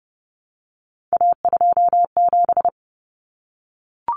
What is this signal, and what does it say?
Morse code, heard as a keyed tone, about 30 wpm, 700 Hz. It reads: A27
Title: Common Cut Numbers - 3-Digit 30wpm